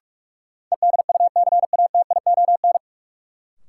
Morse code, heard as a keyed tone, about 46 wpm, 700 Hz. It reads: EDUCATION
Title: Sentences from Top 1100 Words - New Word Drill Alt - 45wpm